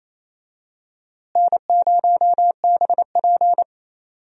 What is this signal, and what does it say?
Morse code, heard as a keyed tone, about 28 wpm, 700 Hz. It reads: N0BP
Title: USA Call Signs - 4 Characters - Repeated 3x - 28wpm